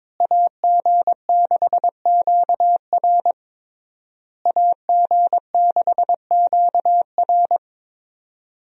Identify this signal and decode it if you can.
Morse code, heard as a keyed tone, about 22 wpm, 700 Hz. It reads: AG6QR AG6QR
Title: USA Call Signs - 5 Characters - Repeated 3x - 22wpm